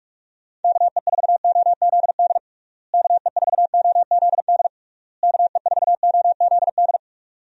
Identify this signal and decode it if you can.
Morse code, heard as a keyed tone, about 42 wpm, 700 Hz. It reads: KE4OZD KE4OZD KE4OZD
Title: USA Call Signs - 6 Characters - Repeated 3x - 45wpm